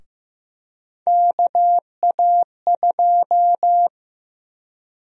Morse code, heard as a keyed tone, about 15 wpm, 700 Hz. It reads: KA2
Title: USA Call Sign Prefixes - 15wpm